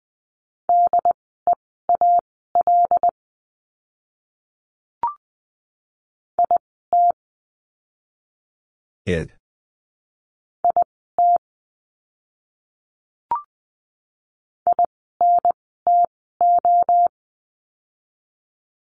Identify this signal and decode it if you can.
Morse code, heard as a keyed tone, about 20 wpm, spaced as for 14 wpm, 700 Hz. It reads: DEAL IT IT INTO
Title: 075 Lesson 13f - Letter L - Words 20wpm